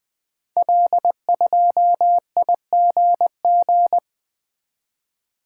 Morse code, heard as a keyed tone, about 20 wpm, 700 Hz. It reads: L2IGG
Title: Code Groups - 5 Letter:Numbers - 20wpm